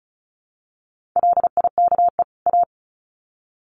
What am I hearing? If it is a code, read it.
Morse code, heard as a keyed tone, about 34 wpm, 700 Hz. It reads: LIKE A
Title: Sets of 2 Words 35wpm